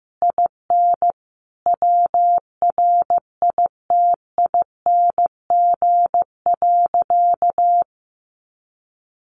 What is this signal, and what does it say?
Morse code, heard as a keyed tone, about 15 wpm, 700 Hz. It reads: IN WRITING.
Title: Sentences from Top 900 Words 15wpm